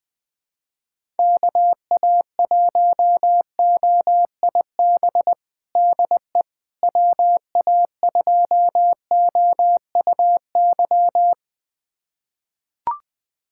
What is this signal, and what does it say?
Morse code, heard as a keyed tone, about 20 wpm, 700 Hz. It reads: KA1OIB DE WA2OUY
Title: USA Callsign DE Callsign 20wpm